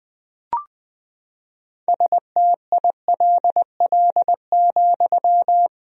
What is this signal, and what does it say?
Morse code, heard as a keyed tone, about 20 wpm, 700 Hz. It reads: STILL,
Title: 2x Word-Spacing - Sentences from Top 300 Words - 20wpm